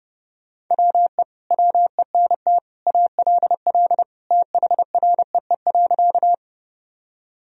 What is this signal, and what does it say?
Morse code, heard as a keyed tone, about 32 wpm, 700 Hz. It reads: WE WENT ALL THREE.
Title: Sentences from Top 1000 Words - Review Part 1 - 30wpm